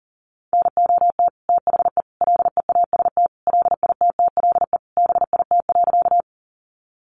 Morse code, heard as a keyed tone, about 37 wpm, 700 Hz. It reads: NOT THE LEAST LITTLE BIT.
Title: Sentences from Top 900 Words 40wpm